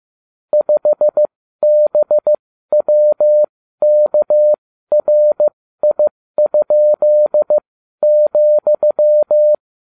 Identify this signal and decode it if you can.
Morse code, heard as a keyed tone, about 15 wpm, spaced as for 12 wpm, 600 Hz. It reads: 5BWKRI?,